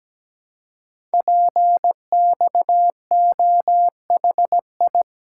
Morse code, heard as a keyed tone, about 17 wpm, 700 Hz. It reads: PXOHI
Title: Code Groups - 5 Letters - 17wpm